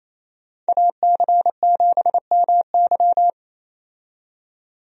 Morse code, heard as a keyed tone, about 28 wpm, 700 Hz. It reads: AC7MY
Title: USA Call Signs - 5 Characters - 28wpm